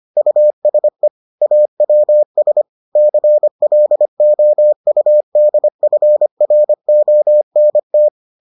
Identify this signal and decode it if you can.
Morse code, heard as a keyed tone, about 25 wpm, 600 Hz. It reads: USE AWS CLOUDFRONT